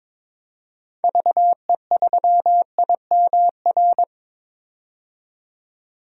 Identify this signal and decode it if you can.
Morse code, heard as a keyed tone, about 22 wpm, 700 Hz. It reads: VE3IMR
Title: DXCC-100 Call Signs - 6 Characters - 22wpm